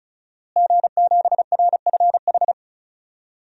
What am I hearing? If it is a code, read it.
Morse code, heard as a keyed tone, about 36 wpm, 700 Hz. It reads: G7RFH
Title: DXCC-100 Call Signs - 5 Characters - Repeated 2x - 35wpm